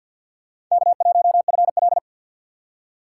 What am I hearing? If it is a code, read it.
Morse code, heard as a keyed tone, about 53 wpm, 700 Hz. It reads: K1FL
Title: USA Call Signs - 4x Character Spacing - 50wpm